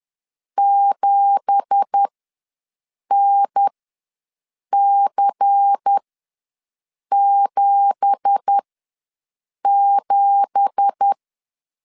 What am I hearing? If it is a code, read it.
Morse code, heard as a keyed tone, about 10 wpm, spaced as for 6 wpm, 800 Hz. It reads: 7NC77
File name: LEZIONI / 9 2867 ws 10wpm 4215